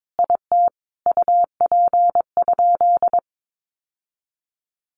Morse code, heard as a keyed tone, about 22 wpm, 700 Hz. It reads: IT UP?
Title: Sentences from Top 1400 Words 22wpm